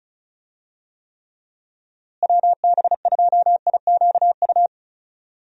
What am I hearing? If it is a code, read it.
Morse code, heard as a keyed tone, about 36 wpm, 700 Hz. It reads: WB2IQU
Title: USA Call Signs - 6 Characters - Repeated 3x - 35wpm